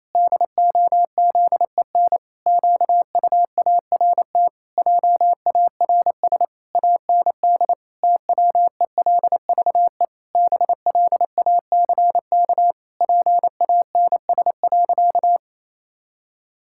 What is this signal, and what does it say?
Morse code, heard as a keyed tone, about 28 wpm, 700 Hz. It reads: DOZEN QUART JARS AND TWELVE BLACK PANS.
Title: Pangram Challenge v2 - 28wpm